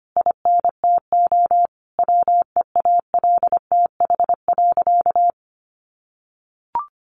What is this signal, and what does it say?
Morse code, heard as a keyed tone, about 25 wpm, 700 Hz. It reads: INTO WEALTH.
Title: Sentences from Top 1400 Words 25wpm